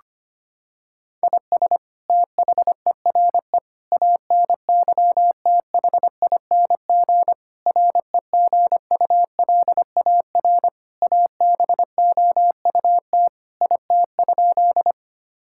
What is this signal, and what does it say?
Morse code, heard as a keyed tone, about 25 wpm, 700 Hz. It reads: IS THERE ANYTHING REGULAR ABOUT IT?
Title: Sentences from Top 1000 Words - Review Part 6 - 25wpm